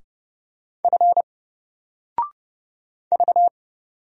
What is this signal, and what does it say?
Morse code, heard as a keyed tone, about 31 wpm, 700 Hz. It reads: F V
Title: Single Letter-Number 30wpm